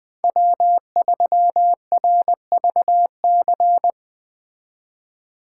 Morse code, heard as a keyed tone, about 19 wpm, 700 Hz. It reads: W3RVC